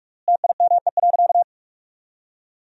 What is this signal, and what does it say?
Morse code, heard as a keyed tone, about 46 wpm, 700 Hz. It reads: TIME.
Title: 3x Word-Spacing - Sentences from Top 400 Words - 45wpm